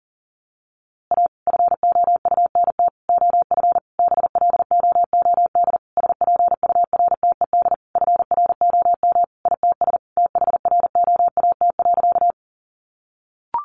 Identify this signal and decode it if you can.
Morse code, heard as a keyed tone, about 40 wpm, 700 Hz. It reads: A FOUNT OF BLOOD SPURTED FROM ITS THROAT.
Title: A Sound Of Thunder 40wpm